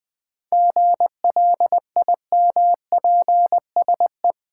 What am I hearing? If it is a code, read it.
Morse code, heard as a keyed tone, about 20 wpm, 700 Hz. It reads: GLIMPSE
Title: 7 Letter Words - First 3 Letters Given - 20wpm